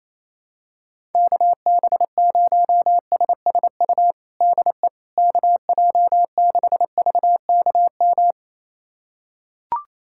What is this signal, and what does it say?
Morse code, heard as a keyed tone, about 28 wpm, 700 Hz. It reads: KB0SSU DE KJ6VKM
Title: USA Callsign DE Callsign - 6 Characters - 28wpm